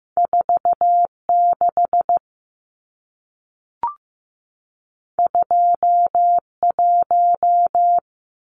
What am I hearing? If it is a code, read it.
Morse code, heard as a keyed tone, about 15 wpm, 700 Hz. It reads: 46 21
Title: Numbers - 2-Digit 15wpm